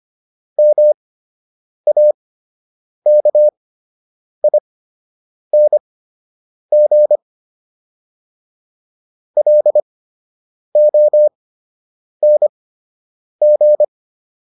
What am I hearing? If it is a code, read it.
Morse code, heard as a keyed tone, about 25 wpm, spaced as for 8 wpm, 600 Hz. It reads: MAKING LONG